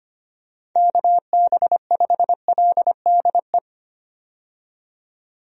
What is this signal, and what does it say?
Morse code, heard as a keyed tone, about 25 wpm, 700 Hz. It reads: KB5LDE
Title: USA Call Signs - 6 Characters - 25wpm